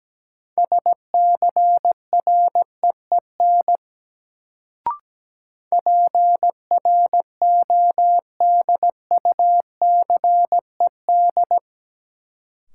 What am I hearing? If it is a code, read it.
Morse code, heard as a keyed tone, about 17 wpm, 700 Hz. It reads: SCREEN PRODUCED
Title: Sentences from Top 1000 Words - Review Part 6 - New Word Drill - 17wpm